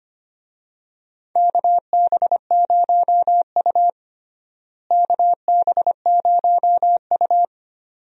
Morse code, heard as a keyed tone, about 25 wpm, 700 Hz. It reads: KB0U KB0U
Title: USA Call Signs - 4 Characters - Repeated 2x - 25wpm